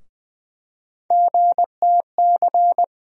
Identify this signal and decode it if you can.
Morse code, heard as a keyed tone, about 20 wpm, 700 Hz. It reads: GTC
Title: USA Call Sign Postfixes - 3 Letters 20wpm